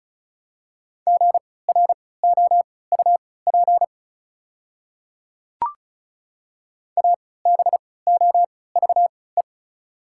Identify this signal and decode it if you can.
Morse code, heard as a keyed tone, about 36 wpm, spaced as for 20 wpm, 690 Hz. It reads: GROUP ABOVE
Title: 3x Character Spacing - Top 500 Words - 35wpm